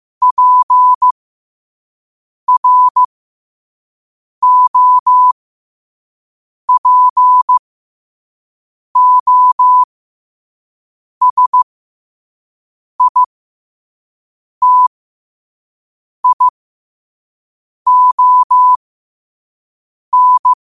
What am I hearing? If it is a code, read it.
Morse code, heard as a keyed tone, about 15 wpm, spaced as for 5 wpm, 1000 Hz. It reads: PROPOSITION
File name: qotdfiles / 5wpm